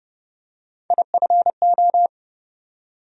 Morse code, heard as a keyed tone, about 30 wpm, 700 Hz. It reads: IFO
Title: USA Call Sign Postfixes - 3 Letters 30wpm